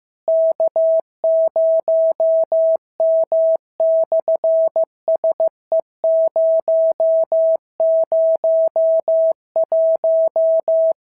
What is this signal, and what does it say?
Morse code, heard as a keyed tone, about 15 wpm, 650 Hz. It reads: K0M/SE001
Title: SOTA QSO Practice 15wpm